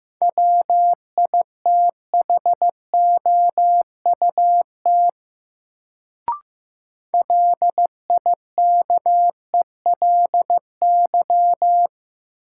Morse code, heard as a keyed tone, about 15 wpm, 700 Hz. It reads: WITHOUT LIKELY
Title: Sentences from Top 600 Words - New Word Drill - 15wpm